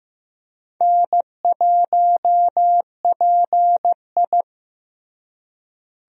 Morse code, heard as a keyed tone, about 15 wpm, 700 Hz. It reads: N1PI
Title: USA Call Signs - 4 Characters - Repeated 2x - 15wpm